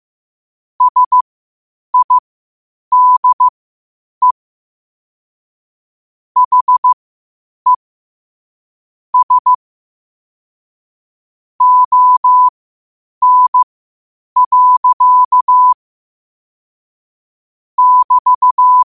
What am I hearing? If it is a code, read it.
Morse code, heard as a keyed tone, about 15 wpm, spaced as for 8 wpm, 1000 Hz. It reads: SIDE HE S ON. <BT>
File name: qotdfiles / 8wpm